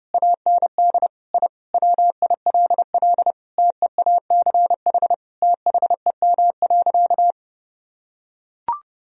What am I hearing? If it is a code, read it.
Morse code, heard as a keyed tone, about 30 wpm, 700 Hz. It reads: AND I WILL TEACH THEM.
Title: Sentences from Top 1000 Words - Review Part 2 - 30wpm